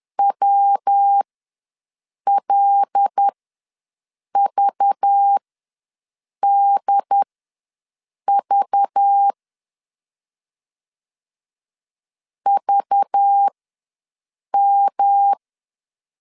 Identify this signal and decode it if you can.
Morse code, heard as a keyed tone, about 10 wpm, spaced as for 6 wpm, 800 Hz. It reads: WLVDV VM
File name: LEZIONI / 5 LBVC / 5 LBV ws 10wpm 3645